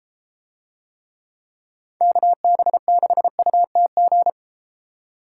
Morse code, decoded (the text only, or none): KB6UTG